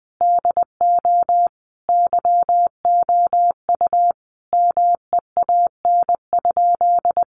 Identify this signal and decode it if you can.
Morse code, heard as a keyed tone, about 20 wpm, 700 Hz. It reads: DO YOU MEAN?
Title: Sentences from Top 200 Words 20wpm